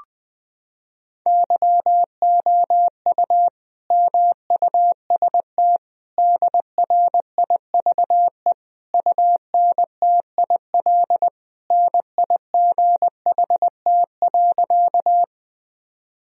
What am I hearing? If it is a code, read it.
Morse code, heard as a keyed tone, about 20 wpm, 700 Hz. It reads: YOU MUST DRIVE UNTIL NIGHT.
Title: Sentences from Top 1000 Words - Review Part 2 - 20wpm